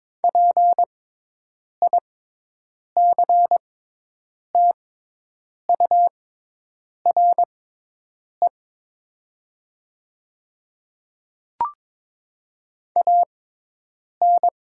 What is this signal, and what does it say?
Morse code, heard as a keyed tone, about 21 wpm, spaced as for 8 wpm, 700 Hz. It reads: PICTURE AN
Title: 6x Character Spacing - Top 300 Words - 22wpm